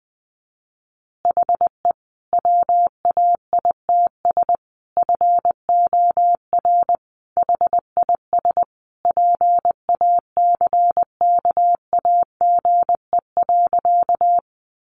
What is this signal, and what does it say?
Morse code, heard as a keyed tone, about 20 wpm, 700 Hz. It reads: HE WAITS FOR HIS PACKAGE.
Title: Additional Easy Phrases 20wpm